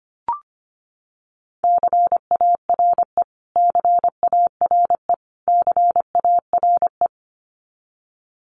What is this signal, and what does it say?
Morse code, heard as a keyed tone, about 25 wpm, 700 Hz. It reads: CARE CARE CARE
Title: Top 500 Words Repeated 25wpm